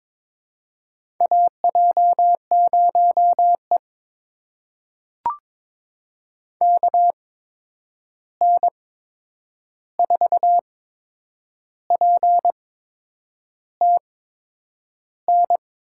Morse code, heard as a keyed tone, about 22 wpm, spaced as for 8 wpm, 700 Hz. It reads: AJ0E KN4PTN
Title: USA Call Signs - 8x Character Spacing - 22wpm